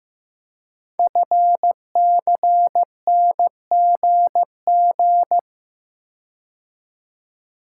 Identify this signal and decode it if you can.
Morse code, heard as a keyed tone, about 15 wpm, 700 Hz. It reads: FCNGG
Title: Code Groups - 5 Letters - 15wpm